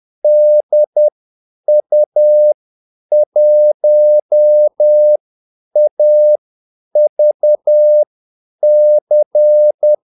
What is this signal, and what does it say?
Morse code, heard as a keyed tone, about 10 wpm, spaced as for 8 wpm, 600 Hz. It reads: DU1AVC
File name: Callsigns-10-8wpm